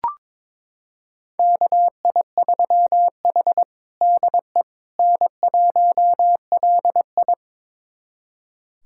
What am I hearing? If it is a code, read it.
Morse code, heard as a keyed tone, about 22 wpm, 700 Hz. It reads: KI3H DE N1LI
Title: USA Callsign DE Callsign - 4 Characters - 22wpm